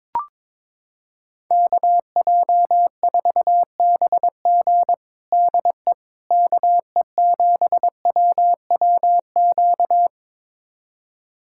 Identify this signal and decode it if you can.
Morse code, heard as a keyed tone, about 22 wpm, 700 Hz. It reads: KJ4BG DE KE7WWQ
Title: USA Callsign DE Callsign 22wpm